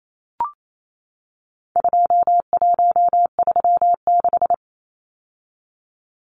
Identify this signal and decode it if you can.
Morse code, heard as a keyed tone, about 28 wpm, 700 Hz. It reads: 2136
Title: Numbers - 4-Digit 28wpm